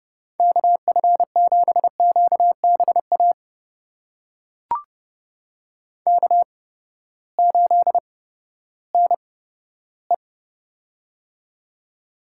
Morse code, heard as a keyed tone, about 30 wpm, spaced as for 15 wpm, 700 Hz. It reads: KF7QBA K8NE
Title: USA Call Signs - 8x Character Spacing - 30wpm